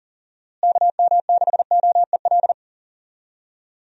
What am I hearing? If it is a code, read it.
Morse code, heard as a keyed tone, about 38 wpm, 700 Hz. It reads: KM6OEL
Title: USA Call Signs - 6 Characters - Repeated 3x - 40wpm